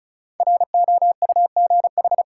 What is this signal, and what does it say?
Morse code, heard as a keyed tone, about 36 wpm, 700 Hz. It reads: ROUGH
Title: Five Letter Words - 35wpm